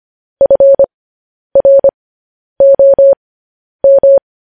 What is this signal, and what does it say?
Morse code, heard as a keyed tone, about 25 wpm, spaced as for 10 wpm, 550 Hz. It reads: FROM
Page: news headlines in morse code at 10 wpm